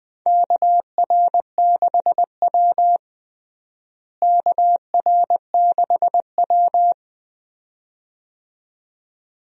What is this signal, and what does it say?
Morse code, heard as a keyed tone, about 21 wpm, 700 Hz. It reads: KR6W KR6W
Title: USA Call Signs - 4 Characters - Repeated 2x - 20wpm